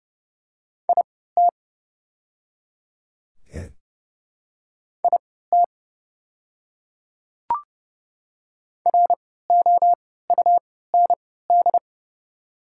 Morse code, heard as a keyed tone, about 30 wpm, spaced as for 17 wpm, 690 Hz. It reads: IT IT ROUND
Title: 3x Character Spacing - Top 200 Words - 30wpm